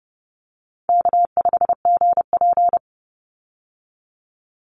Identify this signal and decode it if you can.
Morse code, heard as a keyed tone, about 30 wpm, 700 Hz. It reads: K5GP
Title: Easy Call Signs - 30wpm